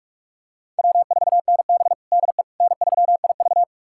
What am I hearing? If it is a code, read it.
Morse code, heard as a keyed tone, about 44 wpm, 700 Hz. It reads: W4NB DE N3IV
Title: USA Callsign DE Callsign - 4 Characters - 45wpm